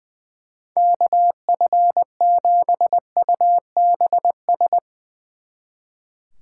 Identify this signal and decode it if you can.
Morse code, heard as a keyed tone, about 20 wpm, 700 Hz. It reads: KF7UBS
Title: Speed Racing - USA 2x3 Call Signs 20wpm